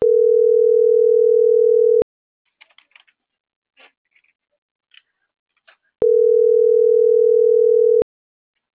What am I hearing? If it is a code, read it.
Ringback tone (North America), 440 + 480 Hz.